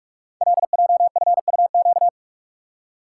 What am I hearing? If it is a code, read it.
Morse code, heard as a keyed tone, about 44 wpm, 700 Hz. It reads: LJFUQ
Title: Code Groups - 5 Letters - 45wpm